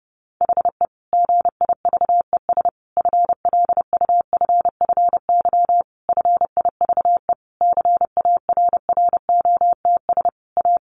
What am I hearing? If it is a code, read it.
Morse code, heard as a keyed tone, about 30 wpm, 700 Hz. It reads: HE GIVES FLUFFY FIVE CARROTS A
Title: Easy Phrases 30wpm